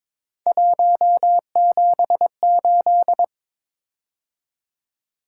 Morse code, heard as a keyed tone, about 22 wpm, 700 Hz. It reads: 178
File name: practice / 22wpm / Three Letter-Number Combinations - 22wpm